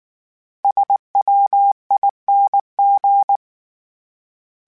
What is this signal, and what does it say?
Morse code, heard as a keyed tone, about 19 wpm, 800 Hz. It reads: SWING
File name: Misc / Misc-mp3s / Words @18